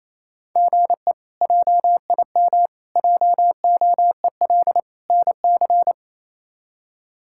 Morse code, heard as a keyed tone, about 28 wpm, 700 Hz. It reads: GE JIM JOEL NC
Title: K1USN SST QSO Practice - 28wpm